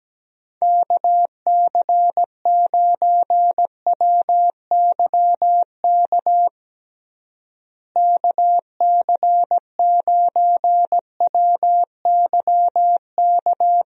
Morse code, heard as a keyed tone, about 17 wpm, 700 Hz. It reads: KC9WYK KC9WYK
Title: USA Call Signs - 6 Characters - Repeated 2x - 17wpm